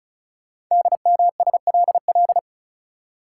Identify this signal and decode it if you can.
Morse code, heard as a keyed tone, about 36 wpm, 700 Hz. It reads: DMSLL
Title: Code Groups - 5 Letter:Numbers - 35wpm